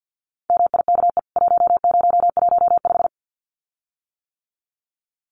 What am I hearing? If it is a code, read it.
Morse code, heard as a keyed tone, about 48 wpm, 700 Hz. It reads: MIKE 1015
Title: CWT Exchange 50wpm